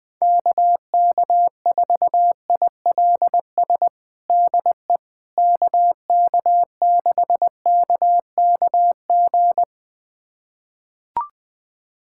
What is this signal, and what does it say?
Morse code, heard as a keyed tone, about 21 wpm, 700 Hz. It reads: KK4ILS DE KK6KKG
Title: USA Callsign DE Callsign - 6 Characters - 20wpm